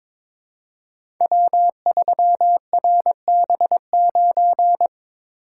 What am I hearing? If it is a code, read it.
Morse code, heard as a keyed tone, about 22 wpm, 700 Hz. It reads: W3RB9
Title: Code Groups - 5 Letter:Numbers - 22wpm